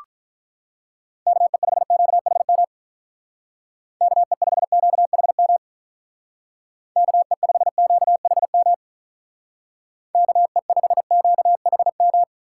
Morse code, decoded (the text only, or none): KE5QHM KE5QHM KE5QHM KE5QHM